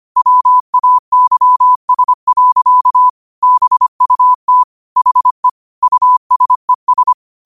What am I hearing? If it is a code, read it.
Morse code, heard as a keyed tone, about 25 wpm, 1000 Hz. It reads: WAYS. BUT HE USES
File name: qotdfiles / 25wpm